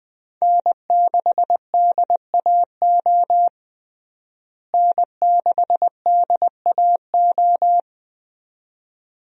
Morse code, heard as a keyed tone, about 21 wpm, 700 Hz. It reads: N6DAO N6DAO
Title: USA Call Signs - 5 Characters - Repeated 2x - 20wpm